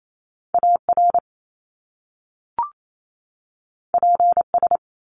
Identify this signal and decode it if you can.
Morse code, heard as a keyed tone, about 28 wpm, 700 Hz. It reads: AR PS
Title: Top Two Letter Combinations in Words 28wpm